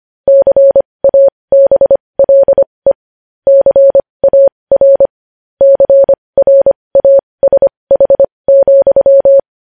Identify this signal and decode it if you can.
Morse code, heard as a keyed tone, about 25 wpm, spaced as for 20 wpm, 550 Hz. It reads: CABLE CAR CRASH,
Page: news-headlines-in-morse-code-at-20-wpm